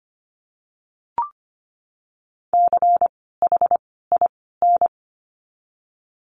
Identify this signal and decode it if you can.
Morse code, heard as a keyed tone, about 25 wpm, 700 Hz. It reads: C H I N
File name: course / 25:10wpm / 094 Lesson 16f - Letter c - Words 25wpm